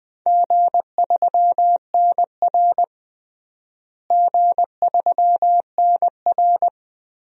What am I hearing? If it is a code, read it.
Morse code, heard as a keyed tone, about 20 wpm, 700 Hz. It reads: G3NR G3NR
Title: DXCC-100 Call Signs - 4 Characters - Repeated 3x - 20wpm